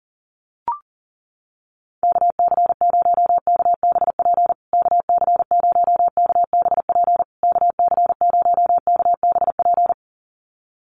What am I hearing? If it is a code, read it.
Morse code, heard as a keyed tone, about 40 wpm, 700 Hz. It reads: KC0KBP KC0KBP KC0KBP
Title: USA Call Signs Repeated 3x 40wpm